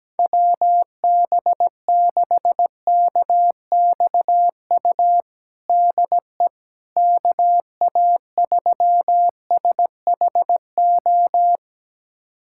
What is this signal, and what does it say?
Morse code, heard as a keyed tone, about 17 wpm, 700 Hz. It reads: WB6KXU DE KA3SHO
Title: USA Callsign DE Callsign 17wpm